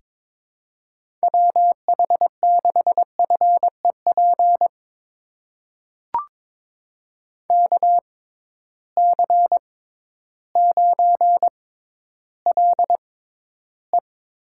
Morse code, heard as a keyed tone, about 22 wpm, spaced as for 12 wpm, 700 Hz. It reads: WH6FEP KC9LE
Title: USA Call Signs - 6x Character Spacing - 22wpm